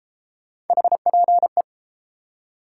Morse code, heard as a keyed tone, about 33 wpm, 700 Hz. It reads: HPE